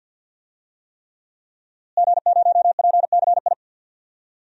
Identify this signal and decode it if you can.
Morse code, heard as a keyed tone, about 46 wpm, 700 Hz. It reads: G0PCI
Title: DXCC-100 Call Signs - 5 Characters - Repeated 3x - 50wpm